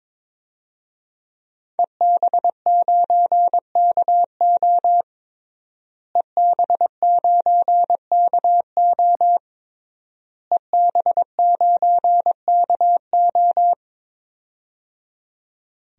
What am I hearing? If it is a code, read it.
Morse code, heard as a keyed tone, about 22 wpm, 700 Hz. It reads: EB9KO EB9KO EB9KO
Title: DXCC-100 Call Signs - 5 Characters - Repeated 3x - 22wpm